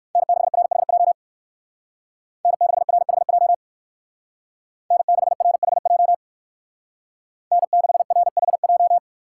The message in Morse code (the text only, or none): N6RHJ N6RHJ N6RHJ N6RHJ